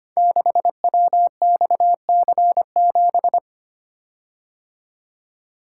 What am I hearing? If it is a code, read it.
Morse code, heard as a keyed tone, about 25 wpm, 700 Hz. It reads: 6WXC7
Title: Code Groups - 5 Letter:Numbers - 25wpm